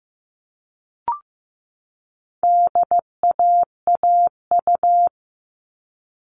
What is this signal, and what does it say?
Morse code, heard as a keyed tone, about 15 wpm, 700 Hz. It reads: DAAU
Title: All Cut Numbers - 4-Digit 15wpm